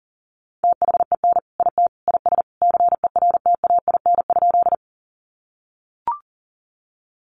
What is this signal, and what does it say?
Morse code, heard as a keyed tone, about 40 wpm, 700 Hz. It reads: THEN IT IS CERTAIN?
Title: Sentences from Top 500 Words 40wpm